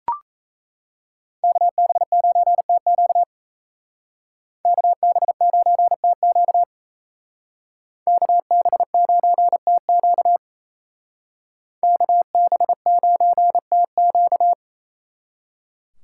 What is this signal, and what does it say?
Morse code, heard as a keyed tone, about 34 wpm, 700 Hz. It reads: KB9TQ KB9TQ KB9TQ KB9TQ
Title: Speed Racing - USA 2x2 Call Signs 28wpm